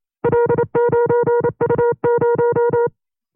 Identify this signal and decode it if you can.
Morse code, heard as a keyed tone, about 28 wpm, 480 Hz. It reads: L9U0
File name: FreeFiles / CW-practice / 25WPM-side1